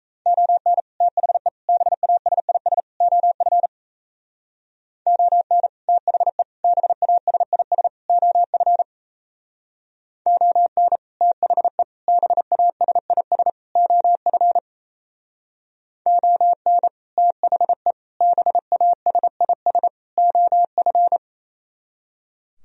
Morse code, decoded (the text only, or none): ON THE BASIS OF ON THE BASIS OF ON THE BASIS OF ON THE BASIS OF